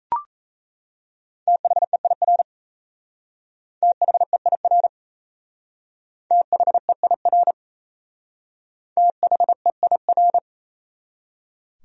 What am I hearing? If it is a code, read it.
Morse code, heard as a keyed tone, about 37 wpm, 700 Hz. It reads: THEIR THEIR THEIR THEIR